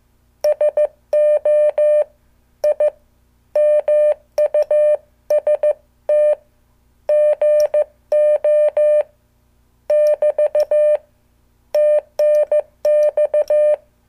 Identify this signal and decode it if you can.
Morse code, heard as a keyed tone, about 15 wpm, 600 Hz. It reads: SO I MUST GO <BT> TNX